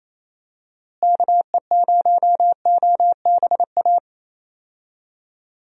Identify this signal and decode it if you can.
Morse code, heard as a keyed tone, about 28 wpm, 700 Hz. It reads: KE0OBA
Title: USA Call Signs - 6 Characters - Repeated 2x - 28wpm